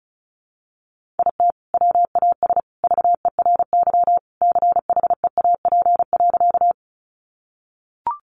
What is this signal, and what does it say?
Morse code, heard as a keyed tone, about 34 wpm, 700 Hz. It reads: IT WAS VERY CHEAP.
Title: Sentences from Top 1200 Words 35wpm